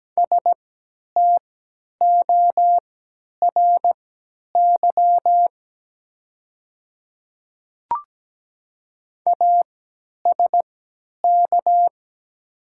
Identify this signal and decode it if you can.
Morse code, heard as a keyed tone, about 17 wpm, spaced as for 10 wpm, 700 Hz. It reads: STORY ASK
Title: 3x Character Spacing - Top 300 Words - 17wpm